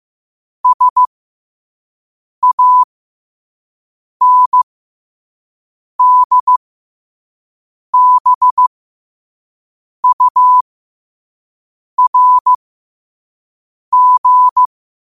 Morse code, heard as a keyed tone, about 15 wpm, spaced as for 5 wpm, 1000 Hz. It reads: SANDBURG
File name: qotdfiles / 5wpm